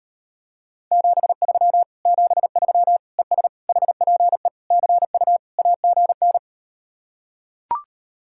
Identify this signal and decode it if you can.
Morse code, heard as a keyed tone, about 38 wpm, 700 Hz. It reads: 73 73 ES HPE CU AGN